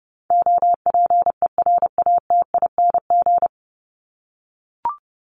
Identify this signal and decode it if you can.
Morse code, heard as a keyed tone, about 30 wpm, 700 Hz. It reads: OPERATING
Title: ING Suffix - Top 100 Words with Base Words 30wpm